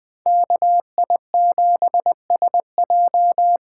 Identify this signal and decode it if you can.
Morse code, heard as a keyed tone, about 20 wpm, 700 Hz. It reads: KI7SJ
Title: USA Call Signs - 6 Characters - 8x Character Spacing - 20wpm